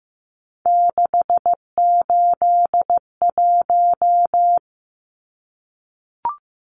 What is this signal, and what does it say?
Morse code, heard as a keyed tone, about 15 wpm, 700 Hz. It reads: 681
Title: Numbers - 3-Digit 15wpm